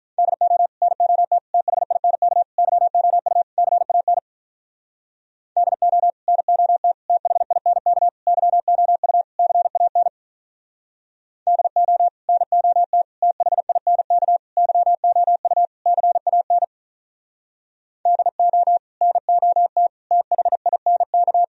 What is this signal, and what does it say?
Morse code, heard as a keyed tone, about 43 wpm, 700 Hz. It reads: DO NOT THINK YOU CAN DO NOT THINK YOU CAN DO NOT THINK YOU CAN DO NOT THINK
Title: Speed-Racing - Sets of 5 Words 35wpm